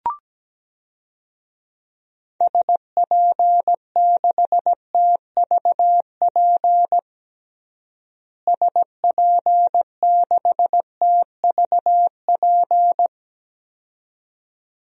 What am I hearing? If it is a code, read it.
Morse code, heard as a keyed tone, about 17 wpm, 700 Hz. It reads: SP6TVP SP6TVP
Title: DXCC-100 Call Signs - 6 Characters - Repeated 2x - 17wpm